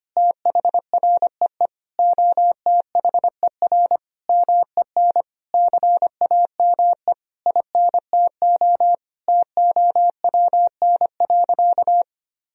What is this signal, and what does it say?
Morse code, heard as a keyed tone, about 25 wpm, 700 Hz. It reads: THREE OTHER MEN CAME INTO TOWN.
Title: Sentences from Top 1000 Words - Review Part 2 - 25wpm